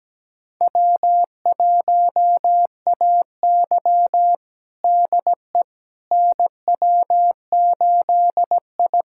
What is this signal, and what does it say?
Morse code, heard as a keyed tone, about 17 wpm, 700 Hz. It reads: W1AY DE NW8I
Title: USA Callsign DE Callsign - 4 Characters - 17wpm